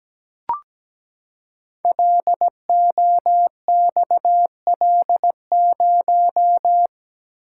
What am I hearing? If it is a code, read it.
Morse code, heard as a keyed tone, about 17 wpm, 690 Hz. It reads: LOXL0
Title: Code Groups - 5 Letter:Numbers - 17wpm